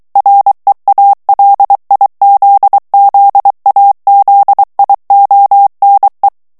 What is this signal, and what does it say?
Morse code, heard as a keyed tone, about 23 wpm, 800 Hz. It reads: REALIZZAZIONE
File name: morse mp3 / ITALIEN texte clair 1 vitesse 22 mots minute